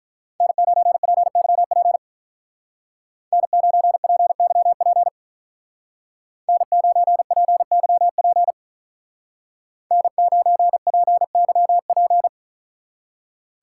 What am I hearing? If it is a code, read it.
Morse code, heard as a keyed tone, about 43 wpm, 700 Hz. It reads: N9PYP N9PYP N9PYP N9PYP
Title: Speed Racing - USA 1x3 Call Signs 35wpm